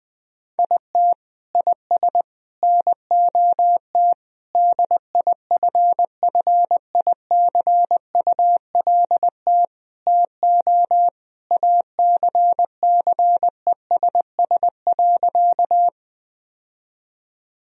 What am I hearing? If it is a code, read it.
Morse code, heard as a keyed tone, about 19 wpm, 700 Hz. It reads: IT IS NOT DIFFICULT TO ACCESS.
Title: Sentences from Top 1000 Words - Review Part 4 - 20wpm